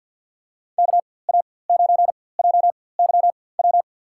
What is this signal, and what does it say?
Morse code, heard as a keyed tone, about 52 wpm, spaced as for 24 wpm, 700 Hz. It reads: KA9JYW
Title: USA Call Signs - 6 Characters - 4x Character Spacing - 50wpm